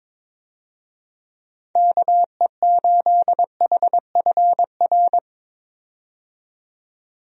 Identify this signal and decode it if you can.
Morse code, heard as a keyed tone, about 22 wpm, 700 Hz. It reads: KE8HFR
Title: USA Call Signs - 6 Characters - 22wpm